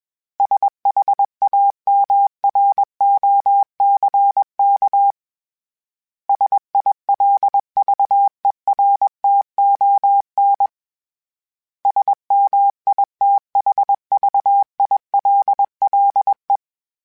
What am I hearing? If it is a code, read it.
Morse code, heard as a keyed tone, about 21 wpm, 800 Hz. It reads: SHAMROCK SILVERTON SMITHVILLE
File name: fromSMs / FromSMs mp3s / Cities@20